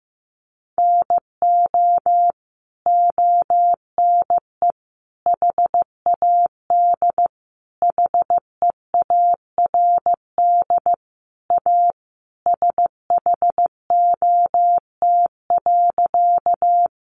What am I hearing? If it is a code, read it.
Morse code, heard as a keyed tone, about 15 wpm, 700 Hz. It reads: NO ONE HAD HEARD A SHOT.
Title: Sentences from Top 1100 Words 15wpm